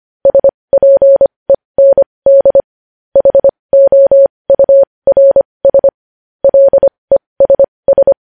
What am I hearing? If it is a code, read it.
Morse code, heard as a keyed tone, about 25 wpm, spaced as for 20 wpm, 550 Hz. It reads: SPEND HOURS LESS
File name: podcast / News headlines-1x-25x20